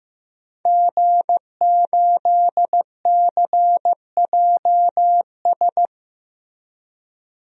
Morse code, heard as a keyed tone, about 15 wpm, 700 Hz. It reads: G8CJS
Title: DXCC-100 Call Signs - 5 Characters - Repeated 3x - 15wpm